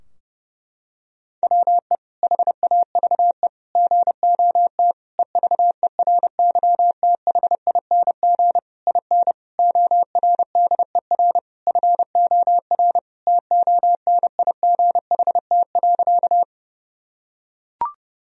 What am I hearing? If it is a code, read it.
Morse code, heard as a keyed tone, about 30 wpm, 700 Hz. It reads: WE HAVE GOT EVERYTHING IN ORDER FOR TONIGHT.
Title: Sentences from Top 1000 Words - Review Part 3 - 30wpm